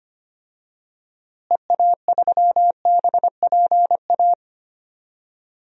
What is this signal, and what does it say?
Morse code, heard as a keyed tone, about 25 wpm, 700 Hz. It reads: EA3BPA
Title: DXCC-100 Call Signs - 6 Characters - Repeated 2x - 25wpm